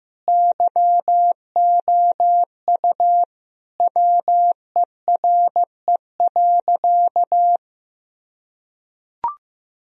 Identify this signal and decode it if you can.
Morse code, heard as a keyed tone, about 15 wpm, 700 Hz. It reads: YOU WERE.
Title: Sentences from Top 1000 Words - Review Part 1 - 15wpm